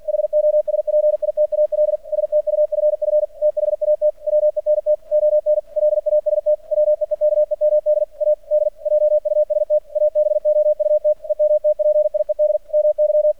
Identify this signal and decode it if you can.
Morse code, heard as a keyed tone, about 50 wpm, 600 Hz. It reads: HOSPITAL STUFF THAT WENT ON LAST WEEKEMD AD JUST ABOUT EMTPIED MY
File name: archived / CRSnet